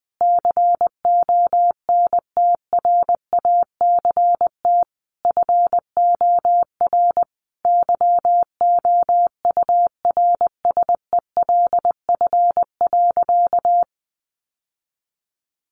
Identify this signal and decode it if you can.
Morse code, heard as a keyed tone, about 20 wpm, 700 Hz. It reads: CONTRACT FOR YOURSELF.
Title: Sentences from Top 1400 Words 20wpm